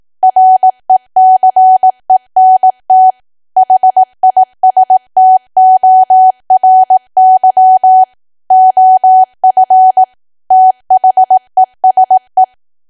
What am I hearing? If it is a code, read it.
Morse code, heard as a keyed tone, about 18 wpm, 750 Hz. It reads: RECENT HISTORY OF THESE
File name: cwa / W1AW-Level2 / 18WPM